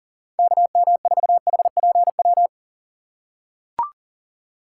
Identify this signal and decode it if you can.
Morse code, heard as a keyed tone, about 40 wpm, 700 Hz. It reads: KM4HPW
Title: USA Call Signs - 6 Characters - 6x Character Spacing - 40wpm